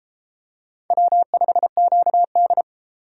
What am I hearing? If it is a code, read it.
Morse code, heard as a keyed tone, about 31 wpm, 700 Hz. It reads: W5QD